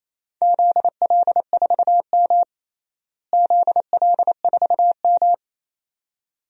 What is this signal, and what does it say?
Morse code, heard as a keyed tone, about 28 wpm, 700 Hz. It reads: ZL4M ZL4M
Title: DXCC-100 Call Signs - 4 Characters - Repeated 2x - 28wpm